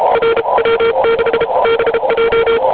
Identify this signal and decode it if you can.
Morse code, heard as a keyed tone, about 32 wpm, spaced as for 28 wpm, 460 Hz, with noise. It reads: RW6BJ
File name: oy 2003 / real-audio / 12m-oy7et-cw-3min